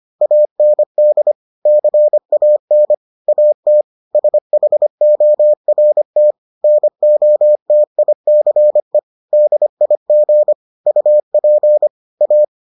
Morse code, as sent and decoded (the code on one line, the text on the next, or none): .- -. -.. / -.-. .- -. / .- - / ... .... --- .-. - / -. --- - .. -.-. / -.. .. --. / ..- .--. / .-
AND CAN AT SHORT NOTICE DIG UP A